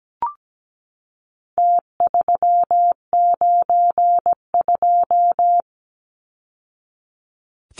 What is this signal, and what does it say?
Morse code, heard as a keyed tone, about 17 wpm, 700 Hz. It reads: T392
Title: Numbers - Contest Style - 4-Digit 17wpm